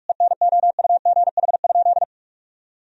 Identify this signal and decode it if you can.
Morse code, heard as a keyed tone, about 46 wpm, 700 Hz. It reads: ENOUGH?
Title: Sentences from Top 1000 Words - Review Part 1 - 45wpm